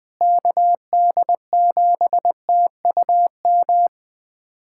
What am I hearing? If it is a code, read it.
Morse code, heard as a keyed tone, about 20 wpm, 700 Hz. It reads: KD7TUM
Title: USA Callsign DE Callsign - 6 Characters - 20wpm